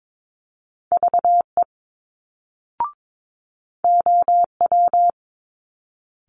Morse code, heard as a keyed tone, about 22 wpm, 700 Hz. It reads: VE OW
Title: Multi-letter Phonemes Quick Fire 22wpm